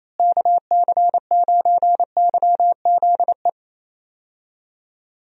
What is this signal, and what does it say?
Morse code, heard as a keyed tone, about 28 wpm, 700 Hz. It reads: KC9YZE
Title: USA Call Signs - 6 Characters - Repeated 3x - 28wpm